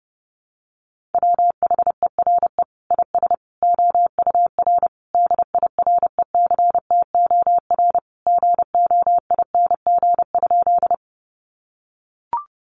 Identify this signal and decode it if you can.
Morse code, heard as a keyed tone, about 30 wpm, 700 Hz. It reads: WHERE IS OUR DIRECTOR GOING?
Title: Sentences from Top 1300 Words 30wpm